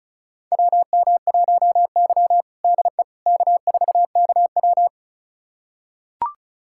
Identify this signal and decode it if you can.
Morse code, heard as a keyed tone, about 36 wpm, 700 Hz. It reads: WM1Y DE K4KW
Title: USA Callsign DE Callsign - 4 Characters - 35wpm